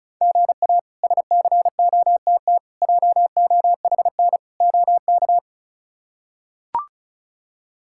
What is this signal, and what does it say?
Morse code, heard as a keyed tone, about 36 wpm, 700 Hz. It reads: GA SCOTT JOHN OK
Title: K1USN SST QSO Practice - 35wpm